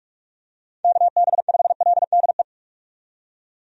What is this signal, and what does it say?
Morse code, heard as a keyed tone, about 46 wpm, 700 Hz. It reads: KB5LDE
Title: USA Call Signs - 6 Characters - 45wpm